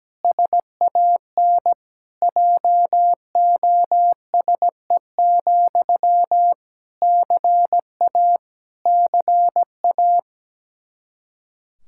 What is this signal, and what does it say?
Morse code, heard as a keyed tone, about 17 wpm, 700 Hz. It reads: SAN JOSE, CA CA
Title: Practice QSO - Exchange 1 - Beginning - 17wpm